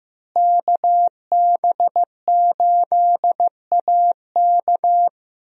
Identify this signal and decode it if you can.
Morse code, heard as a keyed tone, about 15 wpm, 700 Hz. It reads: KB8AK
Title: USA Call Signs - 5 Characters - Repeated 3x - 15wpm